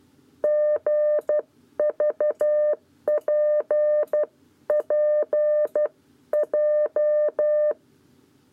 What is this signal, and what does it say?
Morse code, heard as a keyed tone, about 12 wpm, spaced as for 10 wpm, 600 Hz. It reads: GVPPJ
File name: 7b lezione cw 1